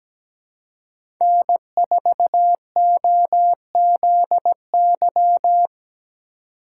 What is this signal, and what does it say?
Morse code, heard as a keyed tone, about 17 wpm, 700 Hz. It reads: N4OZY
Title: USA Call Signs - 5 Characters - 3x Character Spacing - 17wpm